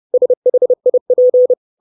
Morse code, heard as a keyed tone, about 30 wpm, 500 Hz. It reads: SHIP
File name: 632 Words 30WPM